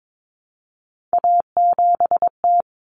Morse code, heard as a keyed tone, about 22 wpm, 700 Hz. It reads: A7T
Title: Common Cut Numbers - 3-Digit 22wpm